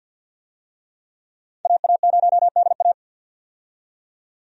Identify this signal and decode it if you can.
Morse code, heard as a keyed tone, about 46 wpm, 700 Hz. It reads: AA0DA
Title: USA Call Signs - 5 Characters - 50wpm